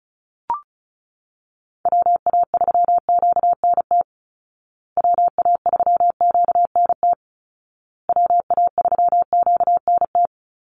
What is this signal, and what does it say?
Morse code, heard as a keyed tone, about 34 wpm, 700 Hz. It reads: WA3QNT WA3QNT WA3QNT
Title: Call Signs Repeated 2x - 35wpm